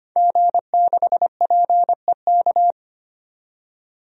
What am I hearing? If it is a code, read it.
Morse code, heard as a keyed tone, about 25 wpm, 700 Hz. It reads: G6PEK
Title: DXCC-100 Call Signs - 5 Characters - 25wpm